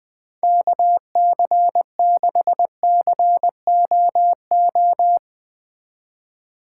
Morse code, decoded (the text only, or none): KC6COO